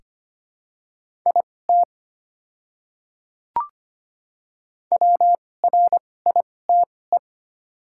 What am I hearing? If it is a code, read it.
Morse code, heard as a keyed tone, about 25 wpm, spaced as for 18 wpm, 700 Hz. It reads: IT WRITE
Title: 2x Character Spacing - Top 100 Words - 25wpm